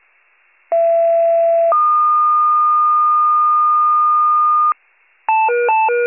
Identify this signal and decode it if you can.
Two-tone page, heard as a one-second tone, then a three-second tone, about 670 then 1220 Hz. It is heard from 0.7 to 4.7 s.